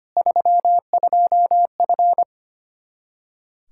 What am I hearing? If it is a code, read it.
Morse code, heard as a keyed tone, about 25 wpm, 700 Hz. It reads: 32F
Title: Practice QSO - Exchange 2 - Info - 25wpm